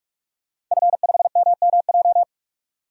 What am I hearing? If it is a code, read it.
Morse code, heard as a keyed tone, about 46 wpm, 700 Hz. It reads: F5MMJ
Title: DXCC-100 Call Signs - 5 Characters - 45wpm